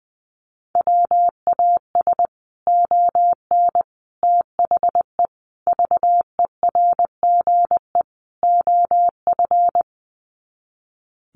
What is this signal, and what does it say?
Morse code, heard as a keyed tone, about 20 wpm, 700 Hz. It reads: WAS ON THE VERGE OF
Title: Sets of 5 Words Encore 20wpm